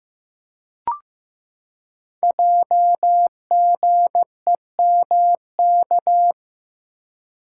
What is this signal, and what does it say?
Morse code, heard as a keyed tone, about 15 wpm, 700 Hz. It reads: JGEMK
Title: Code Groups - 5 Letters - 15wpm